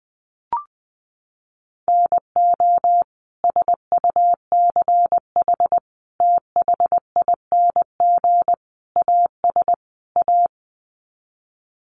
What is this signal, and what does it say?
Morse code, heard as a keyed tone, about 20 wpm, 700 Hz. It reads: NO SUCH THING AS A
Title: Sets of 5 Words Encore 20wpm